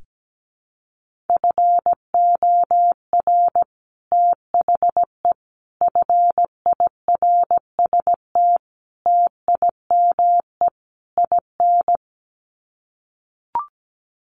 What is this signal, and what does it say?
Morse code, heard as a keyed tone, about 17 wpm, 700 Hz. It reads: FOR THE FIRST TIME IN
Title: Sets of 5 Words 17wpm